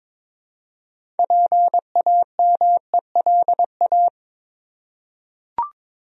Morse code, heard as a keyed tone, about 21 wpm, 700 Hz. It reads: PAMELA
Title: USA Common Names - 22wpm